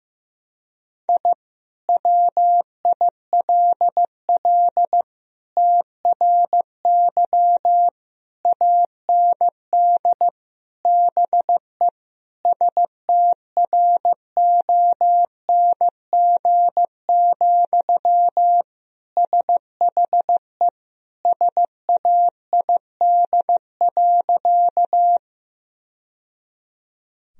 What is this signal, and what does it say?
Morse code, heard as a keyed tone, about 15 wpm, 700 Hz. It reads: I WILL TRY AND BE STRONG, SHE SAID.
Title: Sentences from Top 1000 Words - Review Part 3 - 15wpm